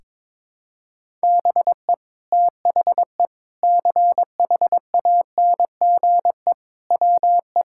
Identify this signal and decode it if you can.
Morse code, heard as a keyed tone, about 22 wpm, 700 Hz. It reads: BE THE CHANGE WE
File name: practice / 22wpm / Sets of 8 Words - 22wpm